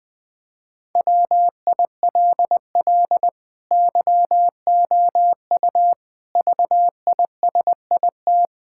Morse code, heard as a keyed tone, about 20 wpm, 700 Hz. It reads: WILL YOU VISIT
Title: Sentences from Top 1000 Words - Review Part 6 - 20wpm